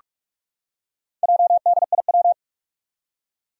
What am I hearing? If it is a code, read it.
Morse code, heard as a keyed tone, about 44 wpm, 700 Hz. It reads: JDIW
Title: Four Challenging Letter-Number Combinations 45wpm